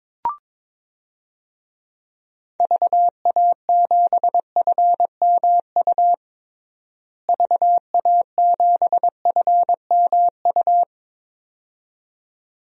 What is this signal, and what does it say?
Morse code, heard as a keyed tone, about 22 wpm, 700 Hz. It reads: VA7FMU VA7FMU
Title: DXCC-100 Call Signs - 6 Characters - Repeated 2x - 22wpm